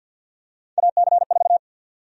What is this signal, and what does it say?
Morse code, heard as a keyed tone, about 49 wpm, 700 Hz. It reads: AC4
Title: USA Call Sign Prefixes - Repeated 2x - 50wpm